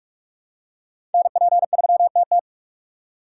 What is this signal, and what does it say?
Morse code, heard as a keyed tone, about 47 wpm, 700 Hz. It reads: NP3TT
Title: USA Call Signs - 5 Characters - 2x Character Spacing - 45wpm